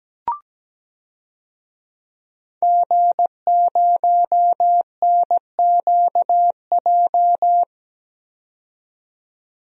Morse code, heard as a keyed tone, about 17 wpm, 700 Hz. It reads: G0NQJ
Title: DXCC-100 Call Signs - 5 Characters - 17wpm